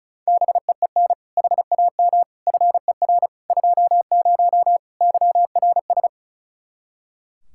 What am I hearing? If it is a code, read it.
Morse code, heard as a keyed tone, about 36 wpm, 700 Hz. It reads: BEEN HAM FER 20 YRS
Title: Practice QSO - Exchange 3 - More Info - 35wpm